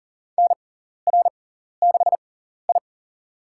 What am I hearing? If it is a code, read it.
Morse code, heard as a keyed tone, about 40 wpm, spaced as for 14 wpm, 700 Hz. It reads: NR6I
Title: USA Call Signs - 4 Characters - 6x Character Spacing - 40wpm